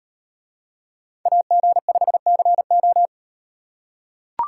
Morse code, heard as a keyed tone, about 38 wpm, 700 Hz. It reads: AG5CO